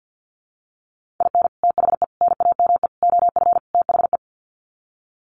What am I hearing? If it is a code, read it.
Morse code, heard as a keyed tone, about 50 wpm, 700 Hz. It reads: IN THE NAME OF THE
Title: Sets of 5 Words Encore 50wpm